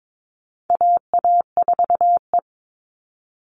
Morse code, heard as a keyed tone, about 22 wpm, 700 Hz. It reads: AA4E
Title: All Cut Numbers - 4-Digit 22wpm